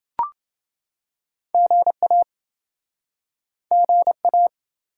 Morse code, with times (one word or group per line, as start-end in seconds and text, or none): 1.5-2.2 GA
3.7-4.5 GA